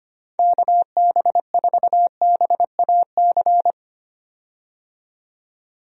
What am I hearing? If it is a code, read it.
Morse code, heard as a keyed tone, about 25 wpm, 700 Hz. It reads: KB4BAC
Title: USA Call Signs - 6 Characters - 25wpm